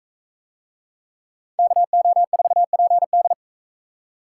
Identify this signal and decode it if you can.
Morse code, heard as a keyed tone, about 43 wpm, 700 Hz. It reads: KO4PD